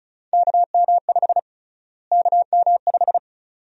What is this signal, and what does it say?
Morse code, heard as a keyed tone, about 36 wpm, 700 Hz. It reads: KM5 KM5
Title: USA Call Sign Prefixes - Repeated 2x - 35wpm